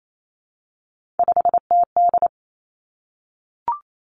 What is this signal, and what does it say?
Morse code, heard as a keyed tone, about 28 wpm, 700 Hz. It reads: 5TD
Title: Three Challenging Letter-Number Combinations 28wpm